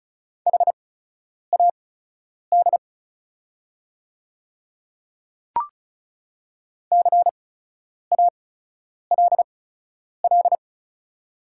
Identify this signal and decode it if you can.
Morse code, heard as a keyed tone, about 36 wpm, spaced as for 10 wpm, 690 Hz. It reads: HAD CALL
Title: 8x Character Spacing - Top 100 Words - 35wpm